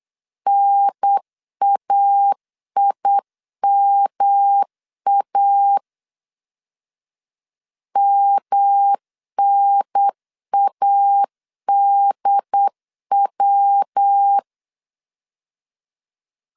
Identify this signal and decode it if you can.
Morse code, heard as a keyed tone, about 8 wpm, 800 Hz. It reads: NAIMA MNADW
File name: CW / DET / 1 3 1500C S C 8wpm 4214